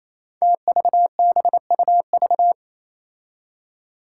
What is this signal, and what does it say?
Morse code, heard as a keyed tone, about 28 wpm, 700 Hz. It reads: TVBUV
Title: Five Challenging Letter-Number Combinations 28wpm